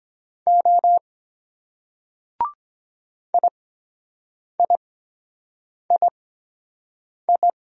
Morse code, heard as a keyed tone, about 23 wpm, spaced as for 7 wpm, 700 Hz. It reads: O IIII